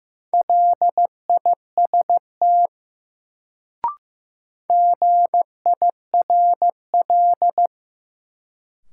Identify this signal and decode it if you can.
Morse code, heard as a keyed tone, about 15 wpm, 700 Hz. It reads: LIST GIRL
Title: Sentences from Top 400 Words - New Word Drill - 15wpm